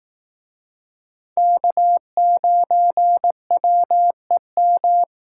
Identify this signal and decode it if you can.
Morse code, heard as a keyed tone, about 19 wpm, 700 Hz. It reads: K9WEM